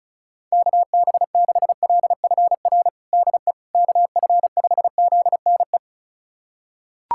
Morse code, heard as a keyed tone, about 34 wpm, 700 Hz. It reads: KB6LFR DE KF5ZNE
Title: USA Callsign DE Callsign 35wpm